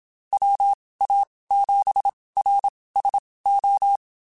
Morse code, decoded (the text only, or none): WA7RSO